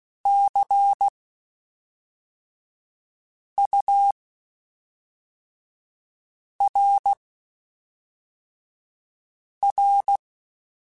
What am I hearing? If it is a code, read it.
Morse code, heard as a keyed tone, about 16 wpm, spaced as for 3 wpm, 800 Hz. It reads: CURR
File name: Spaced Characters / Spaced Characters-mp3s / Weather5 22 sp @5